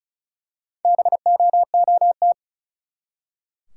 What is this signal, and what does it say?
Morse code, heard as a keyed tone, about 36 wpm, 700 Hz. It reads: BOOT
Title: Four Letter Words - 35wpm